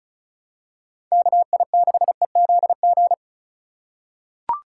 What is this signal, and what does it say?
Morse code, heard as a keyed tone, about 34 wpm, 700 Hz. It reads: KI6EZG
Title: USA Call Signs - 6 Characters - 4x Character Spacing - 35wpm